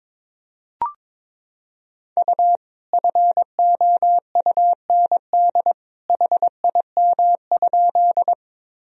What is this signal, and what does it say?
Morse code, heard as a keyed tone, about 22 wpm, 700 Hz. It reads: U FOUND HIM?
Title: Sentences from Top 300 Words and Abbreviations 22wpm